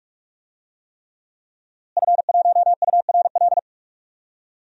Morse code, heard as a keyed tone, about 46 wpm, 700 Hz. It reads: F1URL
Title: DXCC-100 Call Signs - 5 Characters - Repeated 3x - 45wpm